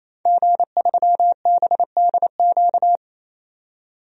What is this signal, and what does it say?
Morse code, heard as a keyed tone, about 28 wpm, 700 Hz. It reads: G3BDQ
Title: DXCC-100 Call Signs - 5 Characters - 28wpm